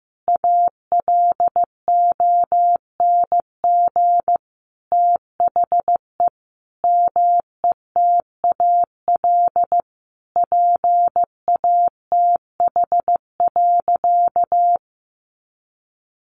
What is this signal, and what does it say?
Morse code, heard as a keyed tone, about 15 wpm, 700 Hz. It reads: ALONG THE METAL PATH.
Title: A Sound Of Thunder 15wpm